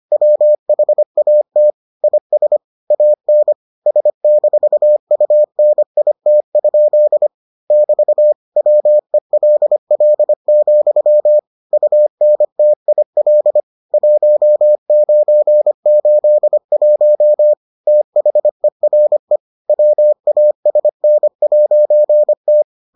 Morse code, heard as a keyed tone, about 25 wpm, 600 Hz. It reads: WHAT IS AN S-UNIT? <BT> WELL, UNTIL 1981 THERE WASN'T